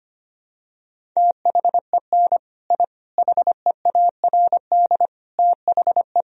Morse code, heard as a keyed tone, about 25 wpm, 700 Hz. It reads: THEN I HEARD THE
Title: Sentences from Top 1000 Words - Review Part 1 - 25wpm